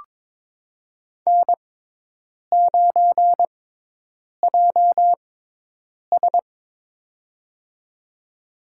Morse code, heard as a keyed tone, about 22 wpm, spaced as for 8 wpm, 700 Hz. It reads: N9JS
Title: USA Call Signs - 4 Characters - 6x Character Spacing - 22wpm